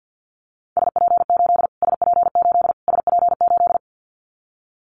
Morse code, heard as a keyed tone, about 50 wpm, 700 Hz. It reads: SP8 SP8 SP8
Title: Callsign Prefixes for Top 25 DXCC - Repeated 3x - 50wpm